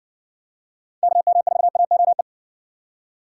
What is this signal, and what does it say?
Morse code, heard as a keyed tone, about 60 wpm, 700 Hz. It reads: KM4AOE